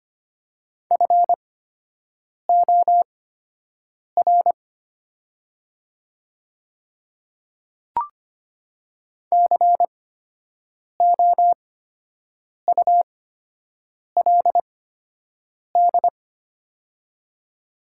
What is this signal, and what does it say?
Morse code, heard as a keyed tone, about 25 wpm, spaced as for 7 wpm, 700 Hz. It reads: FOR COULD
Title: 8x Character Spacing - Top 100 Words - 25wpm